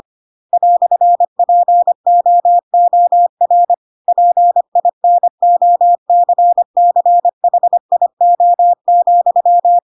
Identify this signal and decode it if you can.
Morse code, heard as a keyed tone, about 25 wpm, 700 Hz. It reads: "POOR PINOCCHIO,
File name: Pinocchio / Pinocchio ch4 25 25 ik1hge